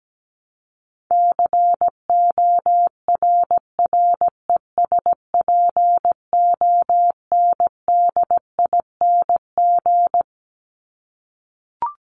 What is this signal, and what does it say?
Morse code, heard as a keyed tone, about 17 wpm, 700 Hz. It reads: CORRESPONDING
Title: ING Suffix - Top 100 Words with Base Words 17wpm